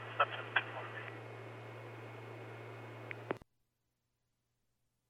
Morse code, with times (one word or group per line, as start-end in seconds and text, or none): none